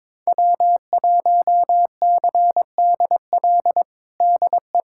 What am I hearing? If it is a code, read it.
Morse code, heard as a keyed tone, about 22 wpm, 700 Hz. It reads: W1CDL DE
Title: USA Callsign DE Callsign - 5 Characters - 22wpm